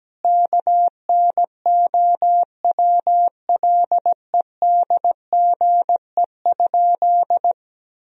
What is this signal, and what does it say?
Morse code, heard as a keyed tone, about 17 wpm, 700 Hz. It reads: KNOWLEDGE?
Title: Sentences from Top 1000 Words - Review Part 3 - 17wpm